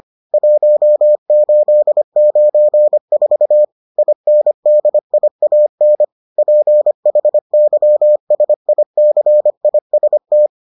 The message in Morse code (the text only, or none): 1894 INDIAN PHYSICIST